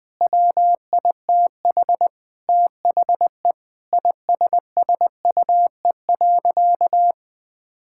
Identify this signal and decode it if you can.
Morse code, heard as a keyed tone, about 20 wpm, 700 Hz. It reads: WITH THE ISSUE.
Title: Sentences with 6-Word Phrases - 20wpm